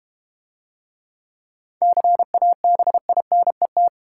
Morse code, heard as a keyed tone, about 31 wpm, 700 Hz. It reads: CABINET
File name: Easy Words - Speed-Racing with Hints - 25wpm